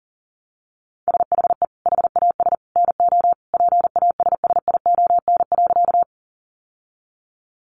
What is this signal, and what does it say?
Morse code, heard as a keyed tone, about 40 wpm, 700 Hz. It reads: SHE HAS NO PASSION.
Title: Sentences from Top 1300 Words 40wpm